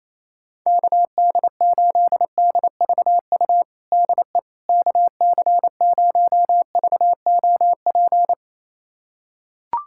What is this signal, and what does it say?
Morse code, heard as a keyed tone, about 28 wpm, 700 Hz. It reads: KD8DVU DE KC0VOP
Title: USA Callsign DE Callsign 28wpm